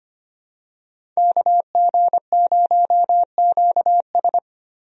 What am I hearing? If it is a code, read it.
Morse code, heard as a keyed tone, about 25 wpm, 700 Hz. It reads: KG0QS
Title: USA Call Signs - 5 Characters - Repeated 2x - 25wpm